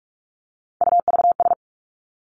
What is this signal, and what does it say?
Morse code, heard as a keyed tone, about 46 wpm, 700 Hz. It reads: UVS
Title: Three Challenging Letter-Number Combinations 45wpm